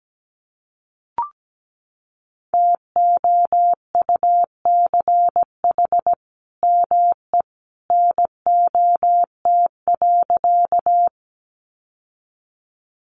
Morse code, heard as a keyed tone, about 17 wpm, 700 Hz. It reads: TOUCH ME NOT.
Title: Sentences from Top 1200 Words 17wpm